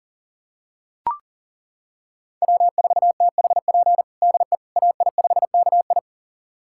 Morse code, heard as a keyed tone, about 40 wpm, 700 Hz. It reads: W4THP DE AI5KI
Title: USA Callsign DE Callsign - 5 Characters - 40wpm